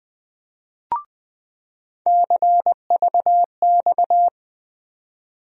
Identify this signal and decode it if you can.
Morse code, heard as a keyed tone, about 20 wpm, 700 Hz. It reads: CVX
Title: USA Call Sign Postfixes - 3 Letters 20wpm